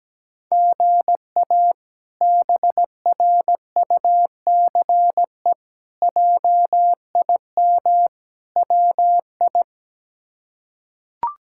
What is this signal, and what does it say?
Morse code, heard as a keyed tone, about 17 wpm, 700 Hz. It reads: GA BRUCE JIM WI
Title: K1USN SST QSO Practice - 17wpm